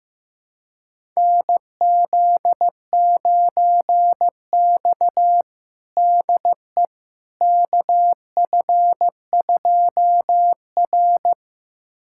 Morse code, heard as a keyed tone, about 15 wpm, 700 Hz. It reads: NZ9X DE KF2R
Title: USA Callsign DE Callsign - 4 Characters - 15wpm